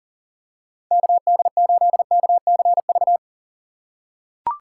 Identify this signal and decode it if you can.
Morse code, heard as a keyed tone, about 40 wpm, 700 Hz. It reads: KD8KCV
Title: USA Call Signs - 6 Characters - 6x Character Spacing - 40wpm